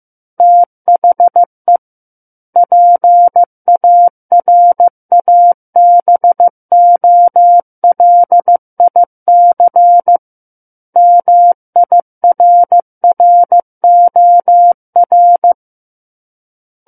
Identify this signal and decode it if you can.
Morse code, heard as a keyed tone, about 15 wpm, 700 Hz. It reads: THE PARABOLIC MIRROR
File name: Book One / 06-Chapter 6 - 15 WPM